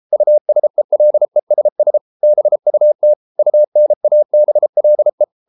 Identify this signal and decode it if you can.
Morse code, heard as a keyed tone, about 33 wpm, 600 Hz. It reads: USELESS BUT UNABLE